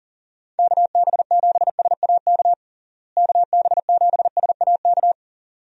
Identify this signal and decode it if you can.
Morse code, heard as a keyed tone, about 37 wpm, 700 Hz. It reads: KB7SAK KB7SAK
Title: USA Call Signs - 6 Characters - Repeated 2x - 40wpm